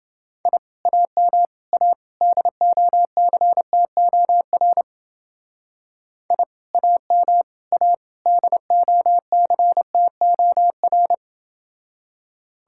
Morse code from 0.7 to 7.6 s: AM A DOCTOR I AM

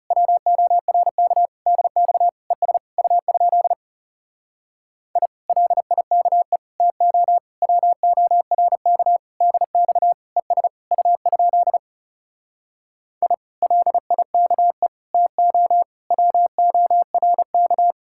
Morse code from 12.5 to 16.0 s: I LIKE TO

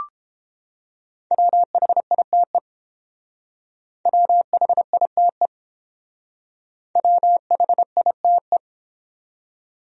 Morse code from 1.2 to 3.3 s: WHITE

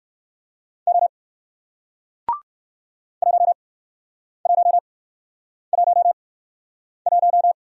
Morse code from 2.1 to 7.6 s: 1111

